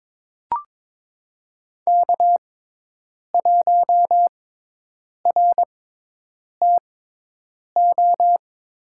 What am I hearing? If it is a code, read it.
Morse code, heard as a keyed tone, about 22 wpm, spaced as for 8 wpm, 700 Hz. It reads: K1RTO
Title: USA Call Signs - 5 Characters - 6x Character Spacing - 22wpm